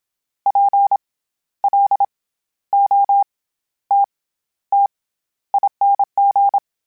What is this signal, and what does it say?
Morse code, heard as a keyed tone, about 27 wpm, 800 Hz. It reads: P L O T T ING
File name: Special INGs-mp3s / Special INGs-mp3s / SlowToFaster ing @25